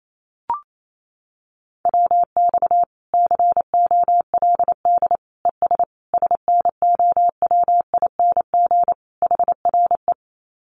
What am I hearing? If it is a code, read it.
Morse code, heard as a keyed tone, about 28 wpm, 700 Hz. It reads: WX COLD ES SNOWING HRE
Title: Common QSO Phrases 28wpm